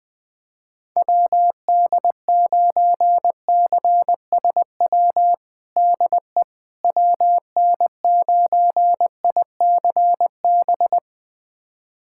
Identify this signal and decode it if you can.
Morse code, heard as a keyed tone, about 20 wpm, 700 Hz. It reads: WD9CSW DE WN9ICB
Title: USA Callsign DE Callsign 20wpm